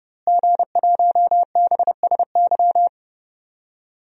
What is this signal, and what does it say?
Morse code, heard as a keyed tone, about 29 wpm, 700 Hz. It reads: G1BSY
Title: DXCC-100 Call Signs - 5 Characters - Repeated 3x - 30wpm